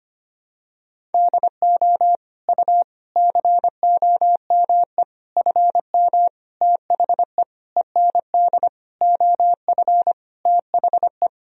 Morse code, heard as a keyed tone, about 25 wpm, 700 Hz. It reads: DO U COME FM THE END OF THE
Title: Sentences from Top 200 Words and Abbreviations 25wpm